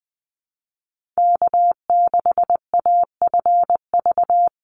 Morse code, heard as a keyed tone, about 20 wpm, 700 Hz. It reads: K6AFV
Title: USA Call Signs 20wpm